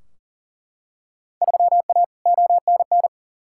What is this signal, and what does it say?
Morse code, heard as a keyed tone, about 37 wpm, 700 Hz. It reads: 3A ONN
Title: Field Day Exchanges Encore - 40wpm